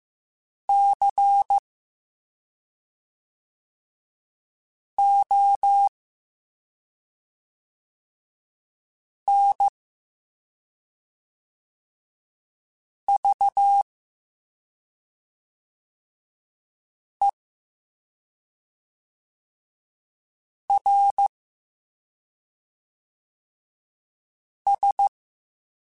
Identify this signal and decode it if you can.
Morse code, heard as a keyed tone, about 15 wpm, 800 Hz. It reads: C O N V E R S